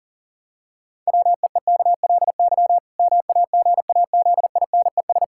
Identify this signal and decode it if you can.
Morse code, heard as a keyed tone, about 40 wpm, 700 Hz. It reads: WEEKLY MAGAZINES
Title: Pangram Challenge v2 - 40wpm